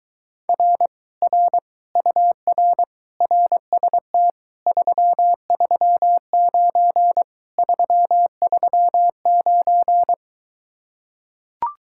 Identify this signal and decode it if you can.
Morse code, heard as a keyed tone, about 23 wpm, 700 Hz. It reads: R R UR RST 339 339